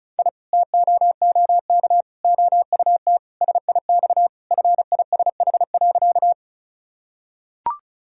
Morse code, heard as a keyed tone, about 36 wpm, 700 Hz. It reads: I TOOK OUT SIX FISH.
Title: Sentences from Top 1000 Words - Review Part 2 - 35wpm